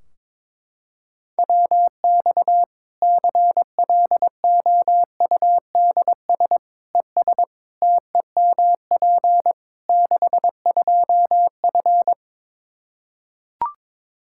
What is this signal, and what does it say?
Morse code, heard as a keyed tone, about 22 wpm, 700 Hz. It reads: WX CLOUDS ES TEMP 62F
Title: K1USN SST QSO Practice 22wpm